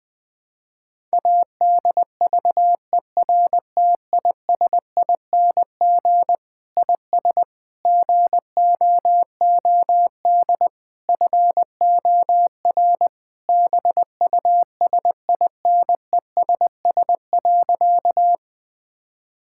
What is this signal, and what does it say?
Morse code, heard as a keyed tone, about 21 wpm, 700 Hz. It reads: ADVERTISING IS GOOD FOR BUSINESS.
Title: Sentences from Top 1000 Words - Review Part 6 - 20wpm